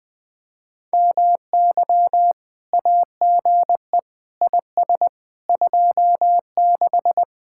Morse code, heard as a keyed tone, about 20 wpm, 700 Hz. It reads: MY AGE IS 26
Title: Speed-Racing Common QSO Phrases - 20wpm